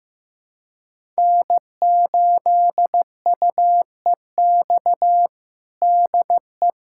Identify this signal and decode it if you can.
Morse code, heard as a keyed tone, about 15 wpm, 700 Hz. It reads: N8UEX DE
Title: USA Callsign DE Callsign - 5 Characters - 15wpm